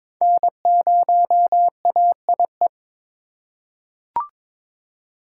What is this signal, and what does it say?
Morse code, heard as a keyed tone, about 21 wpm, 700 Hz. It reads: N0AIE
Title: K1USN SST QSO Practice - 22wpm